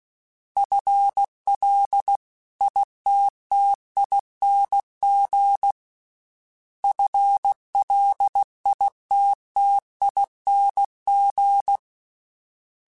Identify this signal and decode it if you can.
Morse code, heard as a keyed tone, about 16 wpm, 800 Hz. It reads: FLITTING FLITTING
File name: FL / FL mp3s / FL Prefixes 3x2 sp @15